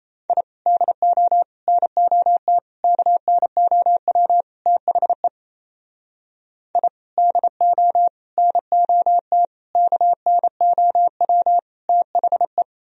Morse code, heard as a keyed tone, about 36 wpm, 700 Hz. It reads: I DO NOT KNOW THE I DO NOT KNOW THE